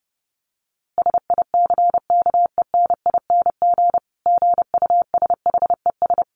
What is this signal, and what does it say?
Morse code, heard as a keyed tone, about 30 wpm, 700 Hz. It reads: SICKENING GUSHES
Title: A Sound Of Thunder 30wpm